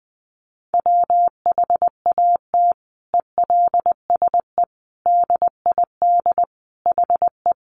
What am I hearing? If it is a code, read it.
Morse code, heard as a keyed tone, about 20 wpm, 700 Hz. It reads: WHAT ELSE DID HE
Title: Sentences from Top 800 Words 20wpm